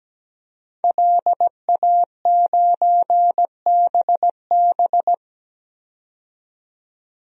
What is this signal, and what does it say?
Morse code, heard as a keyed tone, about 17 wpm, 700 Hz. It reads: LA9BB
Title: DXCC-100 Call Signs - 5 Characters - 17wpm